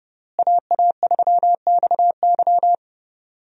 Morse code, heard as a keyed tone, about 32 wpm, 700 Hz. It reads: AA3XY
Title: USA Call Signs - 5 Characters - Repeated 2x - 30wpm